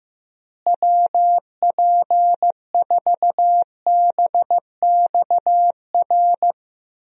Morse code, heard as a keyed tone, about 15 wpm, 700 Hz. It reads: WP4BXR
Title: USA Call Signs - 6 Characters - Repeated 2x - 15wpm